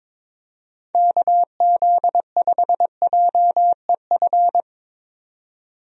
Morse code, heard as a keyed tone, about 22 wpm, 700 Hz. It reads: KZ5JEF
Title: USA Call Signs - 6 Characters - 22wpm